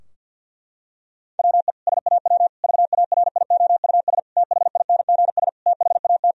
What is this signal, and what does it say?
Morse code, heard as a keyed tone, about 50 wpm, 700 Hz. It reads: WE SAW VARIOUS THINGS THAT
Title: Sentences from Top 1000 Words - Review Part 4 - 50wpm